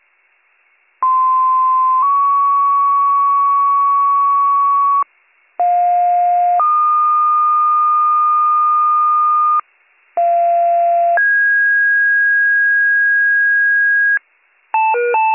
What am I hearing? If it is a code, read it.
Two-tone page, heard as a one-second tone, then a three-second tone, about 1040 then 1120 Hz, about 710 then 1220 Hz, about 690 then 1650 Hz, with noise. It is heard from 1.0 to 14.2 s.